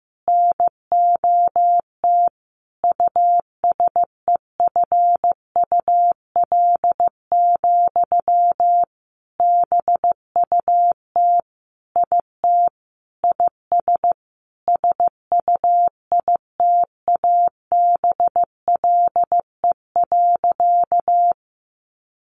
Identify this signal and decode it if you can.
Morse code, heard as a keyed tone, about 15 wpm, 700 Hz. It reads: NOT USEFUL, BUT IT IS SUITABLE.
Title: Sentences from Top 1400 Words 15wpm